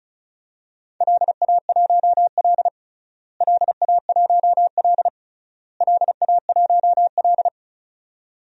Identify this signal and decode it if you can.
Morse code, heard as a keyed tone, about 36 wpm, 700 Hz. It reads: LA1L LA1L LA1L
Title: DXCC-100 Call Signs - 4 Characters - Repeated 3x - 35wpm